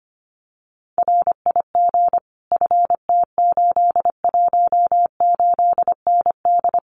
Morse code, heard as a keyed tone, about 25 wpm, 700 Hz. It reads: RIG FT818ND
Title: Common QSO Phrases Encore 25wpm